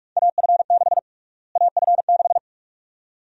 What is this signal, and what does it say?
Morse code, heard as a keyed tone, about 47 wpm, 700 Hz. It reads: AF6 AF6
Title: USA Call Sign Prefixes - Repeated 2x - 45wpm